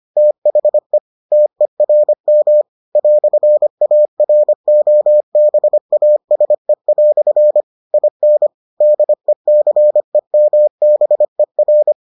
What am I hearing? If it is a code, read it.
Morse code, heard as a keyed tone, about 25 wpm, 600 Hz. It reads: THE TERM "AROBASE" IN DECEMBER